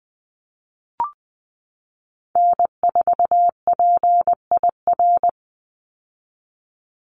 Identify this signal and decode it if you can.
Morse code, heard as a keyed tone, about 20 wpm, 700 Hz. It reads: N4PIR
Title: Call Signs - 20wpm